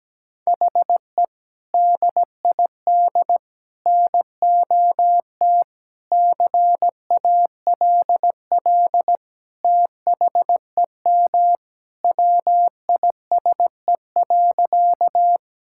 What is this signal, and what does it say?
Morse code, heard as a keyed tone, about 17 wpm, 700 Hz. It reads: HE DID NOT CALL THEM WISE.
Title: Sentences from Top 1600 Words 17wpm